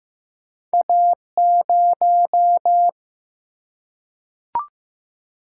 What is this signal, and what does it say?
Morse code, heard as a keyed tone, about 15 wpm, 700 Hz. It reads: A0
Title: Single-Letter Single-Number 15wpm